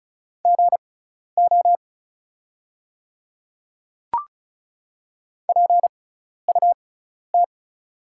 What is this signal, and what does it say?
Morse code, heard as a keyed tone, about 34 wpm, spaced as for 12 wpm, 700 Hz. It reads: GO PUT
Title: 6x Character Spacing - Top 300 Words - 35wpm